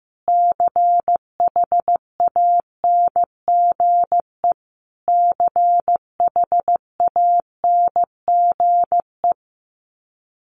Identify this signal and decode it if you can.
Morse code, heard as a keyed tone, about 15 wpm, 700 Hz. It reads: CHANGE CHANGE
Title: Top 200 Words Repeated 15wpm